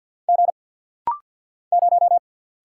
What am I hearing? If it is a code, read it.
Morse code, heard as a keyed tone, about 48 wpm, 690 Hz. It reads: G 0
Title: Single Letter-Number - Rapid-Fire 50wpm